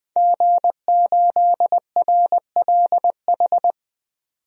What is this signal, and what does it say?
Morse code, heard as a keyed tone, about 20 wpm, 700 Hz. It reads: G8RLH
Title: DXCC-100 Call Signs - 5 Characters - 20wpm